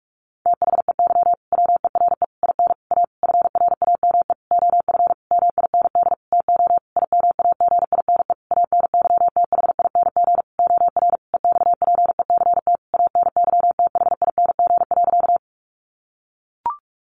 Morse code, heard as a keyed tone, about 44 wpm, 700 Hz. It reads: THEY WERE IN A FRAME OF MIND TO IMAGINE ANYTHING OR EXPECT ANYTHING.
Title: Sentences from Top 1400 Words 45wpm